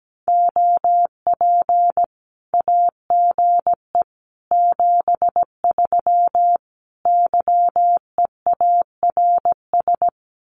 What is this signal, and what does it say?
Morse code, heard as a keyed tone, about 17 wpm, 700 Hz. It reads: OP AGE 73 YEARS
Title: Common QSO Phrases 17wpm